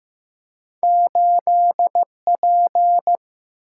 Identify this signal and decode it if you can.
Morse code, heard as a keyed tone, about 15 wpm, 700 Hz. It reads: 8 P
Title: Single-Letter Single-Number 15wpm